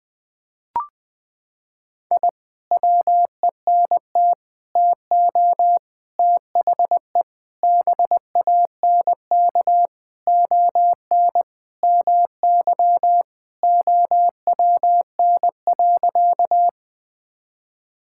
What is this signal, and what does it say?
Morse code, heard as a keyed tone, about 21 wpm, 700 Hz. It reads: I WENT TO THE BANK ON MY OWN.
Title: Sentences from Top 1000 Words - Review Part 6 - 20wpm